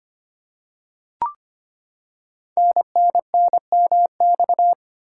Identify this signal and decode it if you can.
Morse code, heard as a keyed tone, about 25 wpm, 700 Hz. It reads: NNNMX
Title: Code Groups - 5 Letters - 25wpm